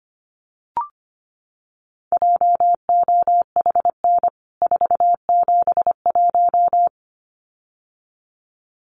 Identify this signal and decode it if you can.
Morse code, heard as a keyed tone, about 25 wpm, 700 Hz. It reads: JOHN 471
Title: CWT Exchange 25wpm